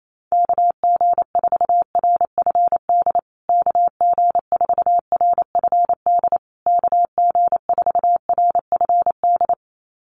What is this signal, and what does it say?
Morse code, heard as a keyed tone, about 28 wpm, 690 Hz. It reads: KG4RFD KG4RFD KG4RFD
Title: USA Call Signs Repeated 3x 28wpm